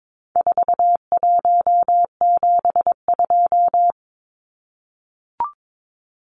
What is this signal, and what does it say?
Morse code, heard as a keyed tone, about 22 wpm, 700 Hz. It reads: 4172
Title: Numbers - 4-Digit 22wpm